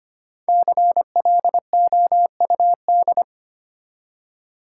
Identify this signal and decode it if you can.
Morse code, heard as a keyed tone, about 25 wpm, 700 Hz. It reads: CLOUD
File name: practice / 25wpm / Easy Words with Hints - 25wpm